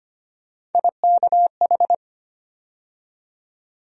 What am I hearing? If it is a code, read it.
Morse code, heard as a keyed tone, about 25 wpm, 700 Hz. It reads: IKH
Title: USA Call Sign Postfixes - 3 Letters 25wpm